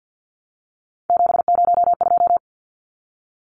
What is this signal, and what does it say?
Morse code, heard as a keyed tone, about 50 wpm, 700 Hz. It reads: 702
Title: Numbers - 3-Digit 50wpm